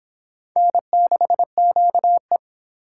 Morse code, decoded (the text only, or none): N6QE